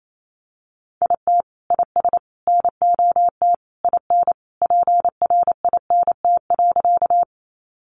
Morse code, heard as a keyed tone, about 28 wpm, 700 Hz. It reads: IT IS NOT IN PRINT.
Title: Sentences from Top 1400 Words 28wpm